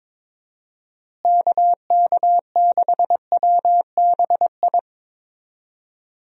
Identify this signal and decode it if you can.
Morse code, heard as a keyed tone, about 22 wpm, 700 Hz. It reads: KK6WBI
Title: USA Call Signs - 6 Characters - Repeated 3x - 22wpm